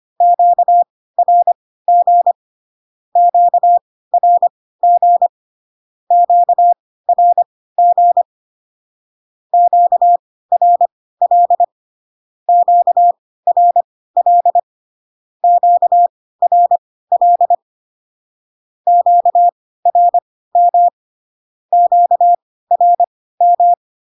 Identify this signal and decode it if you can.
Morse code, heard as a keyed tone, about 25 wpm, spaced as for 16 wpm, 700 Hz. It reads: QRG QRG QRG QRL QRL QRL QRM QRM
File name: cw / Q-codes most common repeated 3 times 10 25wpm ik1hge